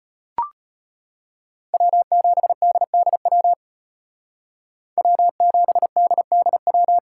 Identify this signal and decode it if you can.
Morse code, heard as a keyed tone, about 32 wpm, 700 Hz. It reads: W7DDW W7DDW